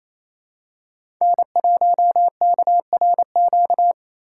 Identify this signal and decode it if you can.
Morse code, heard as a keyed tone, about 28 wpm, 700 Hz. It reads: N1KRQ
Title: USA Call Signs - 5 Characters - 2x Character Spacing - 28wpm